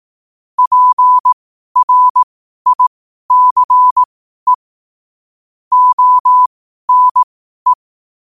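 Morse code, heard as a keyed tone, about 18 wpm, spaced as for 12 wpm, 1000 Hz. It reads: PRICE ONE
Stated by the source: Morse code quote practice at 12 words per minute.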